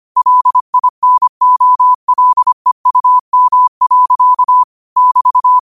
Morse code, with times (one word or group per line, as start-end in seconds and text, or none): 0.2-4.6 LINOLEUM.
5.0-5.6 <BT>